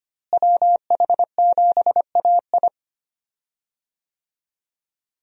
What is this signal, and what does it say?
Morse code, heard as a keyed tone, about 25 wpm, 700 Hz. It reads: WH7AI
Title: USA Call Signs - 5 Characters - Repeated 2x - 25wpm